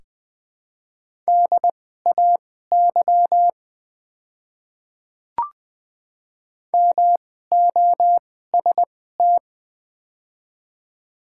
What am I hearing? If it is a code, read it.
Morse code, heard as a keyed tone, about 20 wpm, spaced as for 15 wpm, 700 Hz. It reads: DAY MOST
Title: 2x Character Spacing - Top 100 Words - 20wpm